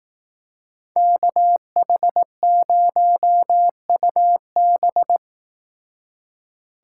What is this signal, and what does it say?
Morse code, heard as a keyed tone, about 18 wpm, 700 Hz. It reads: KH0UB
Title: Speed Racing - USA 2x2 Call Signs 15wpm